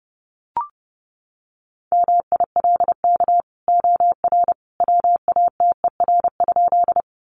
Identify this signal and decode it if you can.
Morse code, heard as a keyed tone, about 30 wpm, 700 Hz. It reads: MILK OR WATER?
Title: Sentences from Top 1500 Words 30wpm